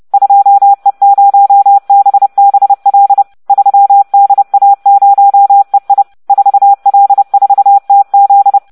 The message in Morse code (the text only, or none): 2E0BBL 3DA0EI 4L4TZ